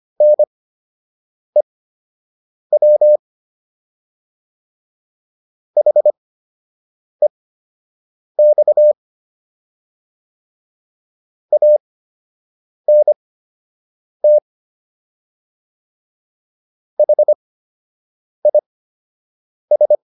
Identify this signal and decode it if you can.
Morse code, heard as a keyed tone, about 25 wpm, spaced as for 7 wpm, 600 Hz. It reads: NEW HEX ANT HIS